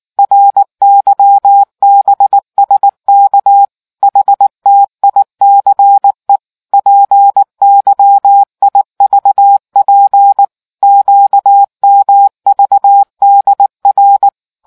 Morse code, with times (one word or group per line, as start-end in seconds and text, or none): 0.2-3.6 RYBSK
4.0-6.4 HTICE
6.7-10.4 PYIVP
10.8-14.3 QMVDR